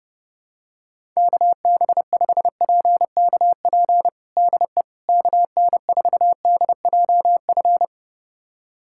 Morse code, heard as a keyed tone, about 30 wpm, 700 Hz. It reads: KB5PKP DE KN4DJF
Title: USA Callsign DE Callsign - 6 Characters - 30wpm